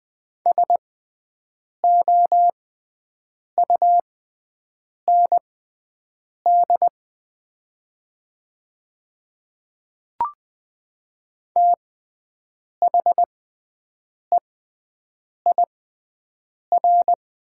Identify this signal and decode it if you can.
Morse code, heard as a keyed tone, about 20 wpm, spaced as for 7 wpm, 700 Hz. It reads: SOUND THEIR
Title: 6x Character Spacing - Top 100 Words - 20wpm